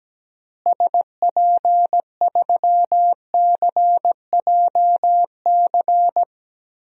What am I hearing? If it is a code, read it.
Morse code, heard as a keyed tone, about 17 wpm, 700 Hz. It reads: SP3CJC
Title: DXCC-100 Call Signs - 6 Characters - Repeated 3x - 17wpm